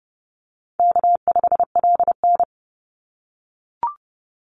Morse code, heard as a keyed tone, about 30 wpm, 700 Hz. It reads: K5LN
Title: Call Signs - 30wpm